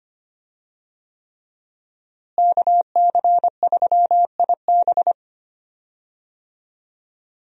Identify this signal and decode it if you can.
Morse code, heard as a keyed tone, about 25 wpm, 700 Hz. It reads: KC3IB
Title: USA Call Signs - 5 Characters - Repeated 3x - 25wpm